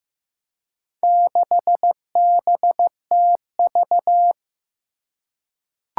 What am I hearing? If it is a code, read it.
Morse code, heard as a keyed tone, about 15 wpm, 700 Hz. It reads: 6BTV
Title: Four Challenging Letter-Number Combinations 15wpm